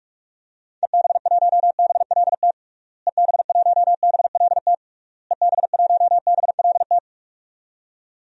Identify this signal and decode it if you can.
Morse code, heard as a keyed tone, about 41 wpm, 700 Hz. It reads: EB1BLT EB1BLT EB1BLT
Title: DXCC-100 Call Signs - 6 Characters - Repeated 3x - 45wpm